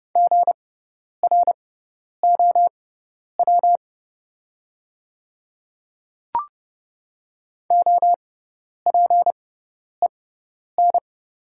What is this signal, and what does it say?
Morse code, heard as a keyed tone, about 30 wpm, spaced as for 10 wpm, 700 Hz. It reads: GROW OPEN
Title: 6x Character Spacing - Top 500 Words - 30wpm